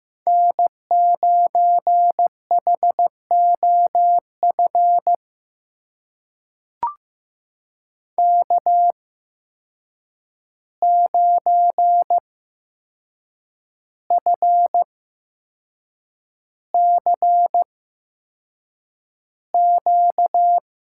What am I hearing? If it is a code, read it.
Morse code, heard as a keyed tone, about 15 wpm, spaced as for 6 wpm, 700 Hz. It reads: N9HOF K9FCQ
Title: USA Call Signs - 5 Characters - 8x Character Spacing - 15wpm